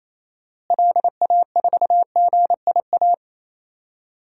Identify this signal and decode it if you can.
Morse code, heard as a keyed tone, about 28 wpm, 700 Hz. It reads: LA4GIA
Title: DXCC-100 Call Signs - 6 Characters - Repeated 2x - 28wpm